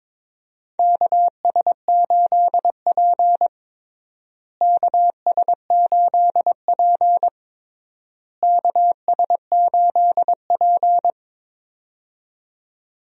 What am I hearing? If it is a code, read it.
Morse code, heard as a keyed tone, about 22 wpm, 700 Hz. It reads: KS8P KS8P KS8P
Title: USA Call Signs - 4 Characters - Repeated 3x - 22wpm